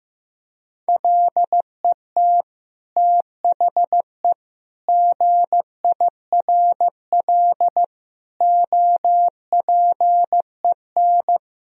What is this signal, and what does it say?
Morse code, heard as a keyed tone, about 15 wpm, 700 Hz. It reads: LET THE GIRL OPEN
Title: Sentences from Top 1000 Words - Review Part 1 - 15wpm